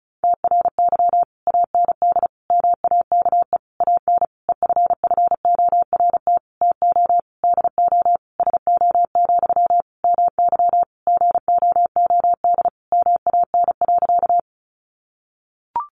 Morse code, heard as a keyed tone, about 34 wpm, 700 Hz. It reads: TRY AND MAKE AN EFFORT TO DO SO, MY GOOD MAN.
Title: Sentences from Top 1200 Words 35wpm